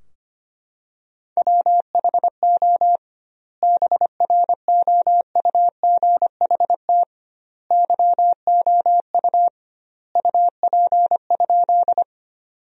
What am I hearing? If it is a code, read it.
Morse code, heard as a keyed tone, about 25 wpm, 700 Hz. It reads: WHO BROUGHT YOU UP?
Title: 2x Word-Spacing - Sentences from Top 500 Words - 25wpm